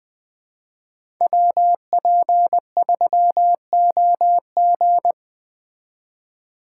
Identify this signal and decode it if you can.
Morse code, heard as a keyed tone, about 20 wpm, 700 Hz. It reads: WP3OG
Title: USA Call Signs - 5 Characters - 4x Character Spacing - 20wpm